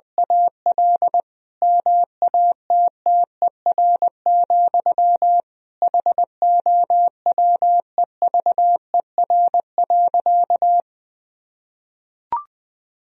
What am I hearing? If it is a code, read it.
Morse code, heard as a keyed tone, about 21 wpm, 700 Hz. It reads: AL MATTER, HOWEVER.
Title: Sentences from Top 1000 Words - Review Part 4 - 20wpm